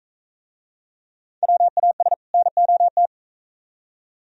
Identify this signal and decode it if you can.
Morse code, heard as a keyed tone, about 44 wpm, 700 Hz. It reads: WAS NOT